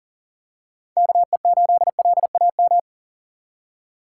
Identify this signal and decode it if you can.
Morse code, heard as a keyed tone, about 40 wpm, 700 Hz. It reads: KE8LAM
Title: USA Call Signs - 6 Characters - 6x Character Spacing - 40wpm